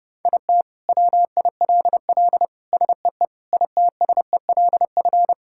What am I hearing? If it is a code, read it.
Morse code, heard as a keyed tone, about 32 wpm, 700 Hz. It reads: IT WILL SEE ITSELF
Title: Sentences from Top 1000 Words - Review Part 5 - 30wpm